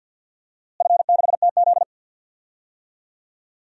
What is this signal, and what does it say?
Morse code, heard as a keyed tone, about 48 wpm, 700 Hz. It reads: F6TZ
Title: DXCC-100 Call Signs - 4 Characters - Repeated 3x - 50wpm